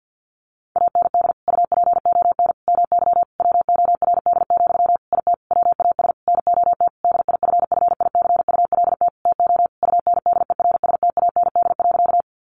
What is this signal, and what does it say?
Morse code, heard as a keyed tone, about 50 wpm, 700 Hz. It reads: AND UPON MY WORD, IT WAS NOT DIFFICULT TO UNDERSTAND.
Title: Sentences from Top 1100 Words 50wpm